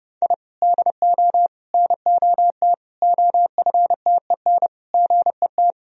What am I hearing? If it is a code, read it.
Morse code, heard as a keyed tone, about 30 wpm, 700 Hz. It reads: I DO NOT OFTEN GET
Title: Sentences from Top 1000 Words - Review Part 1 - 30wpm